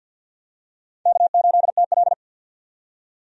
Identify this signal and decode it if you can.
Morse code, heard as a keyed tone, about 48 wpm, 700 Hz. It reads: K8TL
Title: USA Call Signs - 4 Characters - 50wpm